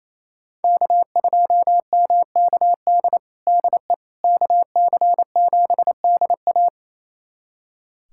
Morse code, heard as a keyed tone, about 28 wpm, 700 Hz. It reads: K2MKD DE KC7DA
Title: USA Callsign DE Callsign - 5 Characters - 28wpm